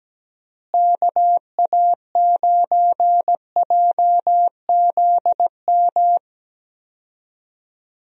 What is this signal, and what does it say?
Morse code, heard as a keyed tone, about 17 wpm, 700 Hz. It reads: KA9JZM
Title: USA Call Signs - 6 Characters - Repeated 2x - 17wpm